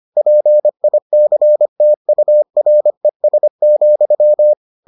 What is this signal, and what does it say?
Morse code, heard as a keyed tone, about 25 wpm, 600 Hz. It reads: PICTURES,